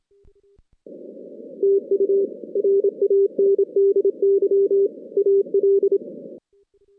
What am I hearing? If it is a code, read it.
Morse code, heard as a keyed tone, about 26 wpm, 400 Hz. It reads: TU RANDY AL
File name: 2016 JAN CW NAQP / 2016 Jan naqp k5ot - 0-10